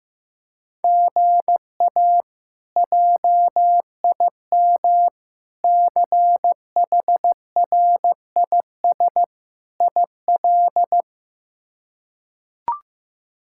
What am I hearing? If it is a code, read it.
Morse code, heard as a keyed tone, about 15 wpm, 700 Hz. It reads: GA JIM CHRIS IL
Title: K1USN SST QSO Practice - 15wpm